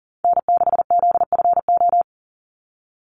Morse code, heard as a keyed tone, about 40 wpm, 700 Hz. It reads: N6ZFO
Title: Call Signs Repeated 3x - 40wpm